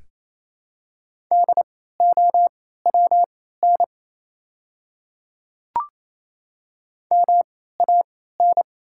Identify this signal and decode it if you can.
Morse code, heard as a keyed tone, about 27 wpm, spaced as for 16 wpm, 700 Hz. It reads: DOWN MAN
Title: 3x Character Spacing - Top 300 Words - 28wpm